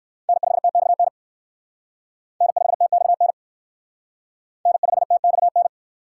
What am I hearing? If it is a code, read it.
Morse code, heard as a keyed tone, about 56 wpm, 700 Hz. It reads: N5TXN N5TXN N5TXN
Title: Speed Racing - USA Call Signs 45wpm